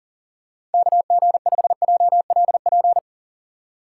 Morse code, heard as a keyed tone, about 40 wpm, 700 Hz. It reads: KG5JLP
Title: USA Call Signs - 6 Characters - 6x Character Spacing - 40wpm